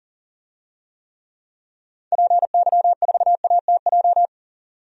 Morse code, heard as a keyed tone, about 40 wpm, 700 Hz. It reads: PY4ATJ
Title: DXCC-100 Call Signs - 6 Characters - 40wpm